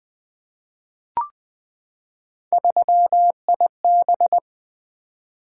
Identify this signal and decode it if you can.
Morse code, heard as a keyed tone, about 20 wpm, 700 Hz. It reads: 3IB
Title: Three Letter-Number Combinations - 20wpm